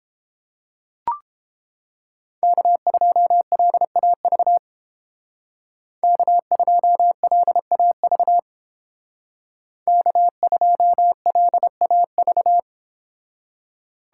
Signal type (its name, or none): Morse code